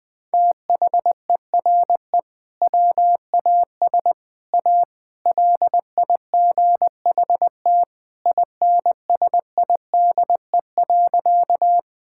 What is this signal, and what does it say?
Morse code, heard as a keyed tone, about 21 wpm, 700 Hz. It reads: THERE WAS A LIGHT INSIDE.
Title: Sentences from Top 1000 Words - Review Part 6 - 20wpm